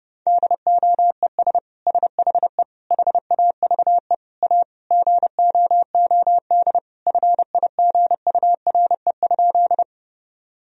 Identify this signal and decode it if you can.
Morse code, heard as a keyed tone, about 30 wpm, 700 Hz. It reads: DOES SHE HAVE A GOOD FIGURE?
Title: Sentences from Top 1000 Words - Review Part 2 - 30wpm